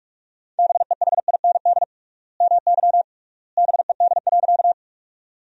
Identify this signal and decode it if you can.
Morse code, heard as a keyed tone, about 42 wpm, 700 Hz. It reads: BEHIND MY BED.
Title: 3x Word-Spacing - Sentences from Top 500 Words - 45wpm